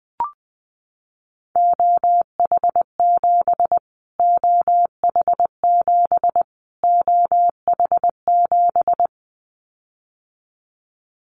Morse code, heard as a keyed tone, about 19 wpm, 700 Hz. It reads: OH7 OH7 OH7
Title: Callsign Prefixes for Top 25 DXCC - Repeated 3x - 20wpm